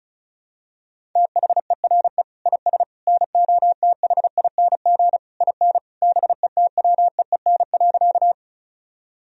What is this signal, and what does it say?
Morse code, heard as a keyed tone, about 34 wpm, 700 Hz. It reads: THERE IS NOTHING IN BETWEEN.
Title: Sentences from Top 1000 Words - Review Part 1 - 35wpm